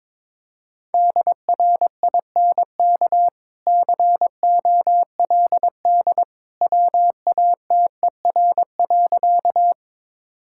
Morse code, heard as a keyed tone, about 22 wpm, 690 Hz. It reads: DRINK COLD WATER.
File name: practice / 22wpm / Easy Sentences with Hints - 22wpm